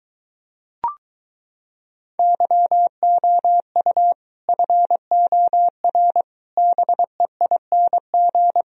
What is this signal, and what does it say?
Morse code, heard as a keyed tone, about 24 wpm, 700 Hz. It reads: YOU FOR BEING